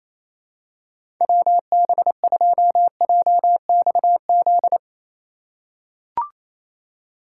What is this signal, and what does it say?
Morse code, heard as a keyed tone, about 28 wpm, 700 Hz. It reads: WB2JXZ
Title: USA Call Signs - 6 Characters - 4x Character Spacing - 28wpm